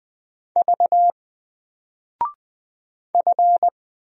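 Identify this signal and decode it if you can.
Morse code, heard as a keyed tone, about 20 wpm, 700 Hz. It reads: V F
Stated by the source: Single letter and number Morse drill at 20 wpm, from N4LCD.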